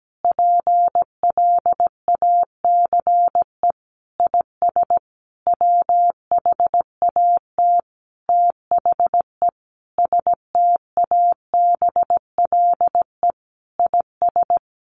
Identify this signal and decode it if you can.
Morse code, heard as a keyed tone, about 17 wpm, 700 Hz. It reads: PLACE IS WHAT THE STABLE IS
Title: Sentences from Top 1500 Words 17wpm